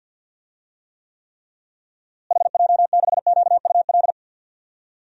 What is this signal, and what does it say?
Morse code, heard as a keyed tone, about 52 wpm, 700 Hz. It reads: HJ6QUL
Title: DXCC-100 Call Signs - 6 Characters - Repeated 3x - 50wpm